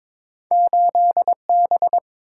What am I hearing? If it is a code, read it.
Morse code, heard as a keyed tone, about 32 wpm, spaced as for 27 wpm, 700 Hz. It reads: 8B
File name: Speed-Racing - Two Letter-Number Combinations - 22wpm